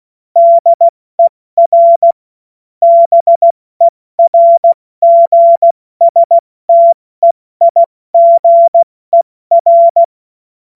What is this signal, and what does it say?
Morse code, heard as a keyed tone, about 16 wpm, 680 Hz. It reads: DER BERGSTEIGER
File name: Morsekurs / Witze / Witz 11. Januar 2022, 70 F80